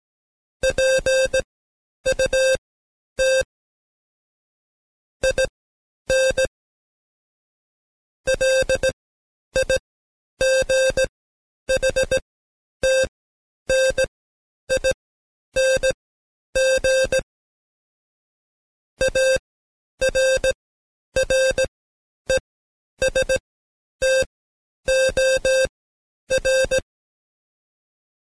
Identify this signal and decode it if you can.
Morse code, heard as a keyed tone, about 17 wpm, spaced as for 10 wpm, 530 Hz. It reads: PUT IN LIGHTNING ARRESTOR
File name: Code Quick / AudioDownld2 / Seg131-Stepper4 9WPM